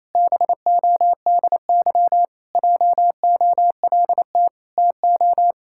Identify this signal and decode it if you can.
Morse code, heard as a keyed tone, about 28 wpm, 700 Hz. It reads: BODY JOLT TO
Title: Pangram Challenge v2 - 28wpm